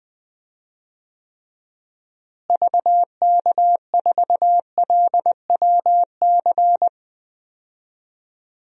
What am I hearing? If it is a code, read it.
Morse code, heard as a keyed tone, about 20 wpm, 700 Hz. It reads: VK4LWC
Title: DXCC-100 Call Signs - 6 Characters - Repeated 3x - 20wpm